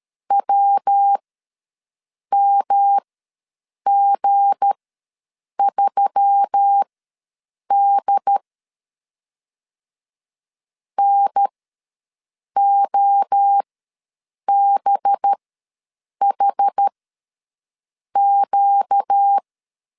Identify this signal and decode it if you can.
Morse code, heard as a keyed tone, about 13 wpm, spaced as for 6 wpm, 800 Hz. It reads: WMG3D NOBHQ
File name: LEZIONI / 8 YJ45 / 8 Y ws 12wpm 3409